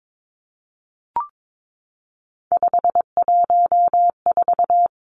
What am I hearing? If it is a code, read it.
Morse code, heard as a keyed tone, about 22 wpm, 700 Hz. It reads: 514
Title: Three Challenging Letter-Number Combinations 22wpm